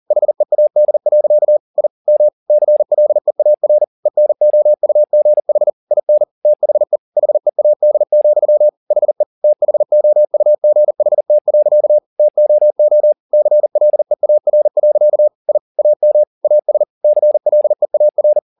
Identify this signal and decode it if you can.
Morse code, heard as a keyed tone, about 40 wpm, 600 Hz. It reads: HEAD. I M CLEAR ENOUGH IN THE HEAD, HE THOUGHT. TOO CLEAR. I AM AS CLEAR
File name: the old man and the sea 2 40wpm